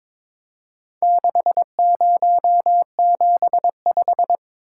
Morse code, heard as a keyed tone, about 22 wpm, 700 Hz. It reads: 6075
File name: practice / 22wpm / Four Letter-Number Combinations - 22wpm